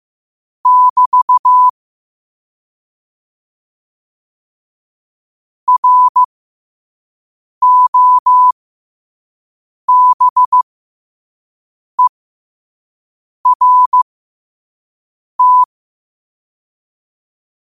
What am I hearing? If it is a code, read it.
Morse code, heard as a keyed tone, about 15 wpm, spaced as for 5 wpm, 1000 Hz. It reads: <BT> ROBERT
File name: qotdfiles / 5wpm